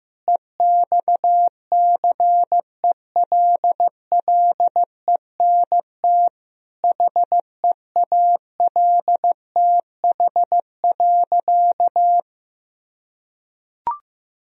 Morse code, heard as a keyed tone, about 15 wpm, 700 Hz. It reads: EXCELLENT HEALTH.
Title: Sentences from Top 1000 Words - Review Part 3 - 15wpm